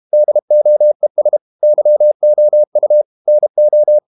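Morse code, heard as a keyed tone, about 32 wpm, 600 Hz. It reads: DOES YOU NO
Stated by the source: Morse code practice at 32 wpm from CW med Gnister, reading Ken Follett's novel Never in English.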